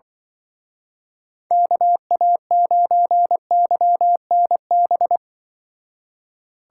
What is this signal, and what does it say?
Morse code, heard as a keyed tone, about 24 wpm, 700 Hz. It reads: KA9YNB
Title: Speed Racing - USA 2x3 Call Signs 20wpm